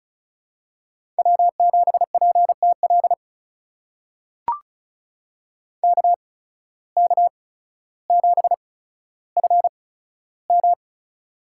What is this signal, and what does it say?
Morse code, heard as a keyed tone, about 34 wpm, spaced as for 15 wpm, 700 Hz. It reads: W7PTL KK7FM
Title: USA Call Signs - 5 Characters - 8x Character Spacing - 35wpm